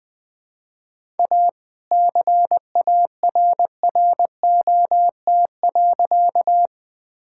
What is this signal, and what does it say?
Morse code, heard as a keyed tone, about 20 wpm, 700 Hz. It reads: A CARROT.
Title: All-Age Appropriate Jokes - 20wpm